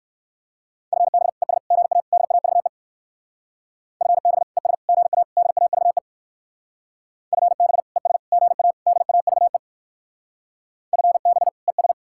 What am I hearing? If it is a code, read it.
Morse code, heard as a keyed tone, about 59 wpm, 700 Hz. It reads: FB ES GA DAVE FB ES GA DAVE FB ES GA DAVE FB ES